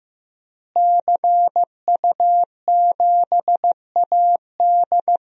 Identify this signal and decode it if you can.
Morse code, heard as a keyed tone, about 15 wpm, 700 Hz. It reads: CU7AD
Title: DXCC-100 Call Signs - 5 Characters - Repeated 3x - 15wpm